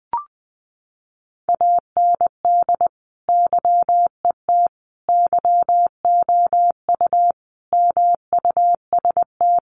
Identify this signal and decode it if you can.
Morse code, heard as a keyed tone, about 20 wpm, 700 Hz. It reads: AND YET YOU MUST
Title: Sentences from Top 1300 Words 20wpm